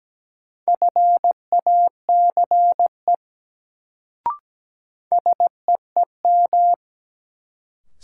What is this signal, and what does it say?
Morse code, heard as a keyed tone, about 17 wpm, 700 Hz. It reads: FACE SEEM
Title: Sentences from Top 300 Words - New Word Drill - 17wpm